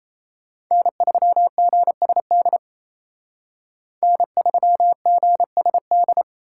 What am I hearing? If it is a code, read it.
Morse code, heard as a keyed tone, about 35 wpm, 700 Hz. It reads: N3GSD N3GSD
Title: Speed Racing - USA 1x3 Call Signs 28wpm